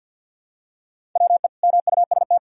Morse code, heard as a keyed tone, about 46 wpm, 700 Hz. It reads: WE MUST
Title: Sentences from Top 1000 Words - Review Part 1 - 50wpm